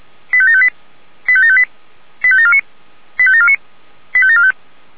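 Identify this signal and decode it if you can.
Five-tone selective call (ZVEI), heard as quick runs of five tones, tones of about 70 ms. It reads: 85758 85758 85749 85749 85745
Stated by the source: Five-tone ZVEI selective call 85827, received on SDR.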